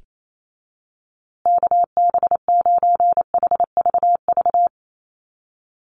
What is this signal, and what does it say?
Morse code, heard as a keyed tone, about 28 wpm, 700 Hz. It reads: KB9HVV
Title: USA Call Signs Repeated 3x 28wpm